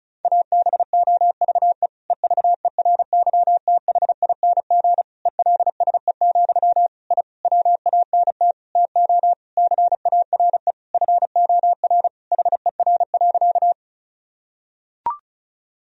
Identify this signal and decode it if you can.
Morse code, heard as a keyed tone, about 36 wpm, 700 Hz. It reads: ABOVE EVERYTHING ELSE, I WANT TO CARE FOR HER.
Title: Sentences from Top 1000 Words - Review Part 2 - 35wpm